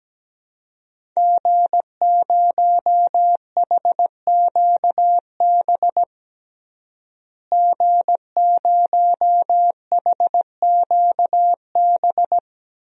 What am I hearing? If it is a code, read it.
Morse code, heard as a keyed tone, about 17 wpm, 700 Hz. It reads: G0HQB G0HQB
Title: DXCC-100 Call Signs - 5 Characters - Repeated 3x - 17wpm